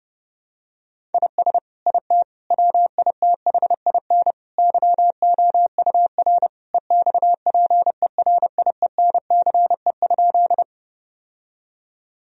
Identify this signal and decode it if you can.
Morse code, heard as a keyed tone, about 32 wpm, 700 Hz. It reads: IS IT WITHIN YOUR EXPERIENCE?
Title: Sentences from Top 1000 Words - Review Part 4 - 30wpm